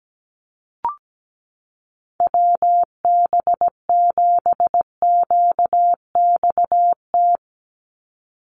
Morse code, heard as a keyed tone, about 17 wpm, 700 Hz. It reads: WB7QXT
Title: USA Call Signs 17wpm